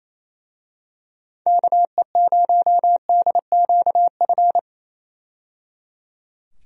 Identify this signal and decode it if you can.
Morse code, heard as a keyed tone, about 28 wpm, 700 Hz. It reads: KE0DQF
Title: USA Call Signs - 6 Characters - 28wpm